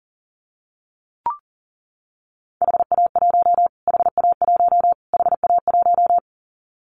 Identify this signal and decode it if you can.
Morse code, heard as a keyed tone, about 40 wpm, 700 Hz. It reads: HA1 HA1 HA1
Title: Callsign Prefixes for Top 25 DXCC - Repeated 3x - 40wpm